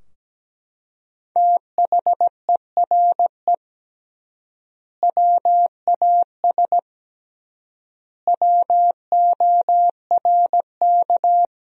Morse code, heard as a keyed tone, about 17 wpm, 700 Hz. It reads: THERE WAS WORK
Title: 3x Word-Spacing - Sentences from Top 200 Words - 17wpm